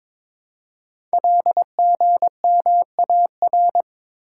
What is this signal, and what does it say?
Morse code, heard as a keyed tone, about 22 wpm, 700 Hz. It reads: LGMAR
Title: Code Groups - 5 Letters - 22wpm